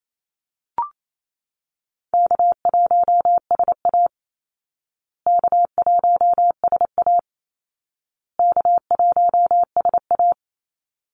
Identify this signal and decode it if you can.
Morse code, heard as a keyed tone, about 28 wpm, 700 Hz. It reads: K1SA K1SA K1SA
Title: Easy Call Signs Repeated 3x - 28wpm